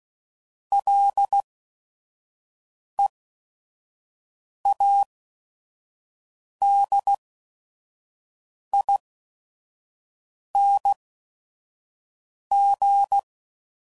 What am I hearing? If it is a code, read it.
Morse code, heard as a keyed tone, about 16 wpm, spaced as for 5 wpm, 800 Hz. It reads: LEADING
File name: Prefixes / FLs & THs / Single L's